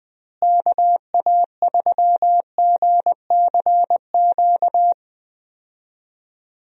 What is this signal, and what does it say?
Morse code, heard as a keyed tone, about 21 wpm, 700 Hz. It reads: KA3GCQ
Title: USA Call Signs - 6 Characters - Repeated 3x - 20wpm